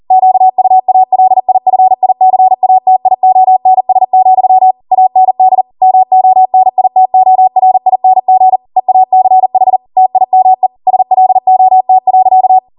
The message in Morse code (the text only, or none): QUALIFICATIONS, AND MONITORING EACH TIME SLOT.